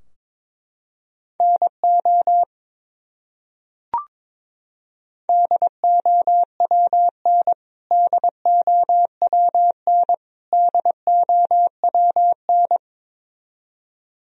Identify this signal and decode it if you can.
Morse code, heard as a keyed tone, about 22 wpm, 700 Hz. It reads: NO DOWN DOWN DOWN
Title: Top 100 Words Repeated 22wpm